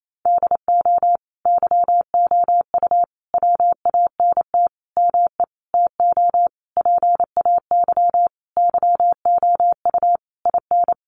Morse code, heard as a keyed tone, about 28 wpm, 700 Hz. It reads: DO YOU WANT ME TO PAY YOU IN
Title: Sentences from Top 1400 Words 28wpm